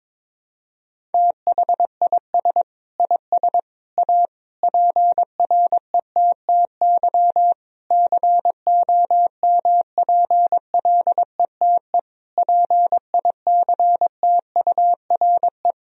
Morse code, heard as a keyed tone, about 22 wpm, 700 Hz. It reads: THIS IS A PRETTY COMPLETE PICTURE
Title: Sentences from Top 1000 Words - Review Part 2 - 22wpm